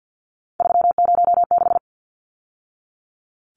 Morse code, heard as a keyed tone, about 50 wpm, 700 Hz. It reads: 306
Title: Numbers - 3-Digit 50wpm